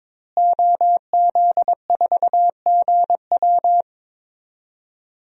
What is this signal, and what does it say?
Morse code, heard as a keyed tone, about 22 wpm, 700 Hz. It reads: OZ4GW
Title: DXCC-100 Call Signs - 5 Characters - Repeated 2x - 22wpm